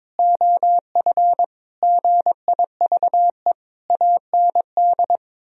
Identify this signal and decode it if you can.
Morse code, heard as a keyed tone, about 22 wpm, 700 Hz. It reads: OF GIVE AND
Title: Sentences from Top 1000 Words - Review Part 4 - 22wpm